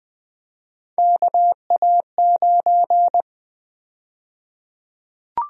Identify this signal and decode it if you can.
Morse code, heard as a keyed tone, about 20 wpm, 700 Hz. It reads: KA9
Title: USA Call Sign Prefixes - Repeated 2x - 20wpm